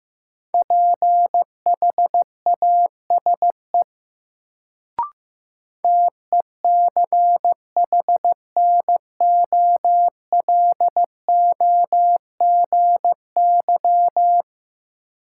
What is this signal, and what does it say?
Morse code, heard as a keyed tone, about 15 wpm, 700 Hz. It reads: PHASE TECHNOLOGY
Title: Sentences from Top 1200 Words - New Word Drill Alt - 15wpm